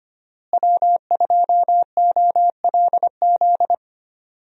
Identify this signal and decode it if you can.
Morse code, heard as a keyed tone, about 25 wpm, 700 Hz. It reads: W2OLZ
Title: USA Call Signs - 5 Characters - Repeated 3x - 25wpm